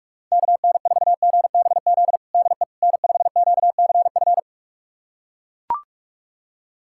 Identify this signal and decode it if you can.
Morse code, heard as a keyed tone, about 41 wpm, 690 Hz. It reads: KN4GBZ DE N5QCF
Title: USA Callsign DE Callsign 45wpm